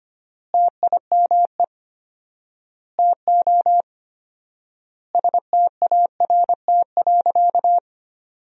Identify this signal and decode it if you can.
Morse code, heard as a keyed tone, about 25 wpm, 700 Hz. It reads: TIME TO START.
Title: 4x Word-Spacing - Sentences from Top 300 Words - 25wpm